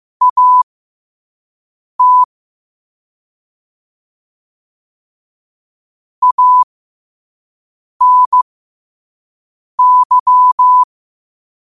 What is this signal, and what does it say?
Morse code, heard as a keyed tone, about 15 wpm, spaced as for 5 wpm, 1000 Hz. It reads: AT ANY
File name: qotdfiles / 5wpm